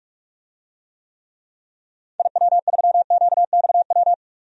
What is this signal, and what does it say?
Morse code, heard as a keyed tone, about 46 wpm, 700 Hz. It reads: IW3QXW
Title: DXCC-100 Call Signs - 6 Characters - 45wpm